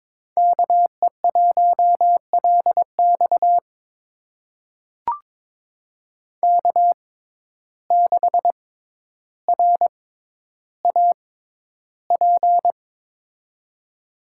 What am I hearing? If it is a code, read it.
Morse code, heard as a keyed tone, about 22 wpm, spaced as for 11 wpm, 700 Hz. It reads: KE1LX K6RAP
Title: USA Call Signs - 6x Character Spacing - 22wpm